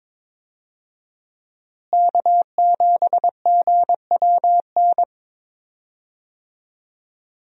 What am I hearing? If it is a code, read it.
Morse code, heard as a keyed tone, about 22 wpm, 700 Hz. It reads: K7GWN
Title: USA Call Signs - 5 Characters - Repeated 3x - 22wpm